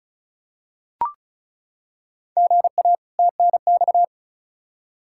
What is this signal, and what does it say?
Morse code, heard as a keyed tone, about 36 wpm, 700 Hz. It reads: GA TNX
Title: Practice QSO - Exchange 1 - Beginning - 35wpm